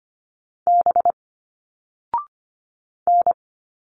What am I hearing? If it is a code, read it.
Morse code, heard as a keyed tone, about 25 wpm, 700 Hz. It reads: B N
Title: Single Letters 25wpm